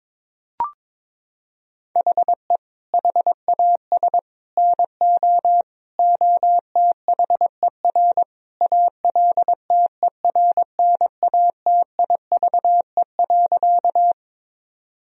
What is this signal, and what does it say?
Morse code, heard as a keyed tone, about 22 wpm, 700 Hz. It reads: HE HAS NO OTHER ALTERNATIVE.
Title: Sentences from Top 1000 Words - Review Part 4 - 22wpm